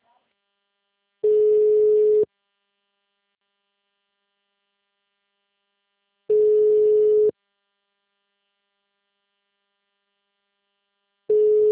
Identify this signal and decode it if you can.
Ringback tone (Europe, 425 Hz), with speech.
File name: out- -2002- - - .46